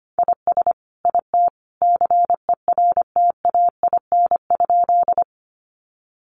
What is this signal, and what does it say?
Morse code, heard as a keyed tone, about 25 wpm, 700 Hz. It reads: IS IT CERTAIN?
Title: Sentences from Top 500 Words 25wpm